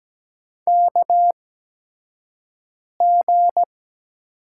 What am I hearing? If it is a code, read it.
Morse code, heard as a keyed tone, about 17 wpm, spaced as for 5 wpm, 700 Hz. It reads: KG
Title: USA Call Signs - 6 Characters - 8x Character Spacing - 17wpm